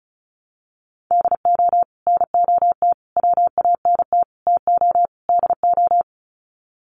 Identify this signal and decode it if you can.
Morse code, heard as a keyed tone, about 34 wpm, 700 Hz. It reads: DO NOT WANT TO DO
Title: Sets of 5 Words 35wpm